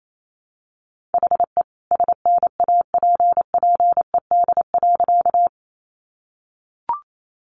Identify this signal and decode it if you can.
Morse code, heard as a keyed tone, about 28 wpm, 700 Hz. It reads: HE SNAPPED.
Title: A Sound Of Thunder 28wpm